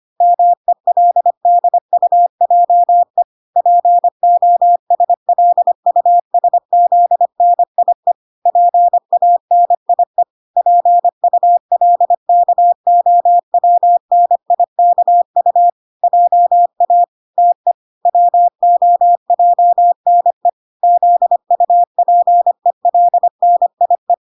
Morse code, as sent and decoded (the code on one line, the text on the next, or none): -- . .-.. -.. ..- .--- . / .--. --- ... .-.. ..- ... --.. -. .. / .--. .- -. .. / .--. ..- .-.. -.- --- .-- -. .. -.- ..- / .--- .- / - . / .-- --- .--- -. / --.. ..- .--. .-.. -. .. .
MELDUJE POSLUSZNIE PANIE PULKOWNIKU JA TE WOJNE ZUPELNIE